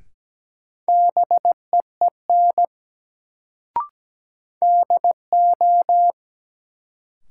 Morse code, heard as a keyed tone, about 17 wpm, 700 Hz. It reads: BEEN DO
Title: Sets of 2 Words - New Word Drill - 17wpm